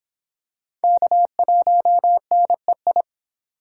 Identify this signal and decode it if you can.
Morse code, heard as a keyed tone, about 24 wpm, 700 Hz. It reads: K1NEI